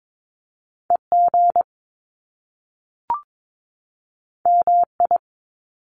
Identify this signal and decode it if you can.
Morse code, heard as a keyed tone, about 22 wpm, 700 Hz. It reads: EG MI
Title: Top Two Letter Combinations in Words 22wpm